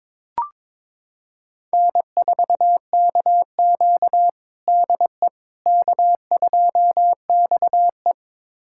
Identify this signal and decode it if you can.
Morse code, heard as a keyed tone, about 21 wpm, 700 Hz. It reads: N4KQ DE K2XE
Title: USA Callsign DE Callsign - 4 Characters - 22wpm